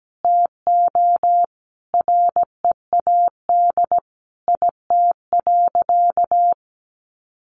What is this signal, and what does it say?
Morse code, heard as a keyed tone, about 17 wpm, 700 Hz. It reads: TO READ IT.
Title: Sentences from Top 500 Words 17wpm